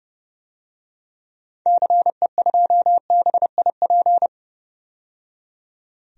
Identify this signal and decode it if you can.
Morse code, heard as a keyed tone, about 30 wpm, 700 Hz. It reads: CE2BIP
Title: DXCC-100 Call Signs - 6 Characters - 30wpm